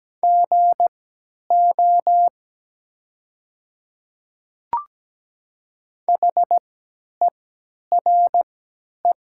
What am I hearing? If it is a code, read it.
Morse code, heard as a keyed tone, about 17 wpm, spaced as for 10 wpm, 700 Hz. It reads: GO HERE
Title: 3x Character Spacing - Top 500 Words - 17wpm